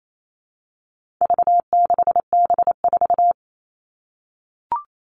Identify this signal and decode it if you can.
Morse code, heard as a keyed tone, about 28 wpm, 700 Hz. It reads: V6B4
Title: All Cut Numbers - 4-Digit 28wpm